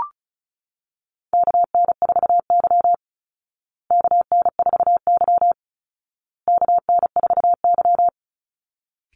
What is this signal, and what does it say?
Morse code, heard as a keyed tone, about 34 wpm, 700 Hz. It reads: KN4Y KN4Y KN4Y
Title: Call Signs Repeated 2x - 35wpm